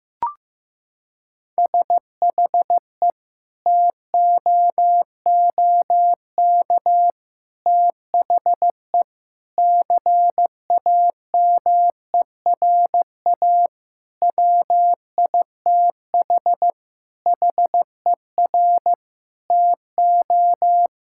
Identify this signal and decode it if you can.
Morse code, heard as a keyed tone, about 15 wpm, 700 Hz. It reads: SHE TOOK THE CAMERA WITH HER TO
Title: Sentences from Top 1000 Words - Review Part 5 - 15wpm